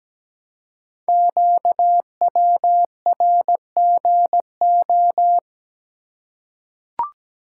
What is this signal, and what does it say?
Morse code, heard as a keyed tone, about 17 wpm, 700 Hz. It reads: QWRGO
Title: Code Groups - 5 Letters - 17wpm